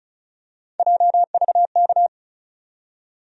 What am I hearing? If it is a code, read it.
Morse code, heard as a keyed tone, about 34 wpm, 700 Hz. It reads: JVK
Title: USA Call Sign Postfixes - 3 Letters 35wpm